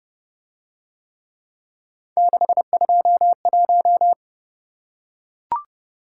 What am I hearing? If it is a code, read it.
Morse code, heard as a keyed tone, about 29 wpm, 700 Hz. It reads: 621